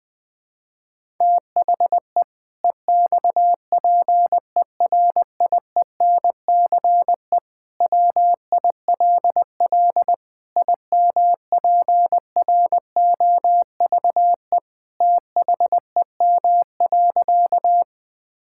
Morse code, heard as a keyed tone, about 20 wpm, 700 Hz. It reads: THE EXPERIENCE WILL IMPROVE THEM.
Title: Sentences from Top 1000 Words - Review Part 5 - 20wpm